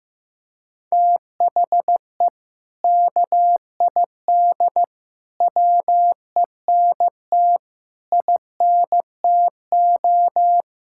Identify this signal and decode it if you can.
Morse code, heard as a keyed tone, about 15 wpm, 700 Hz. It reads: THE KID WENT INTO
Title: Sentences from Top 1600 Words 15wpm